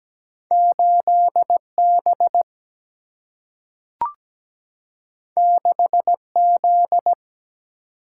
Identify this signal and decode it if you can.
Morse code, heard as a keyed tone, about 17 wpm, 700 Hz. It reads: 8 B 6 Z
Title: Single-Letter Single-Number 17wpm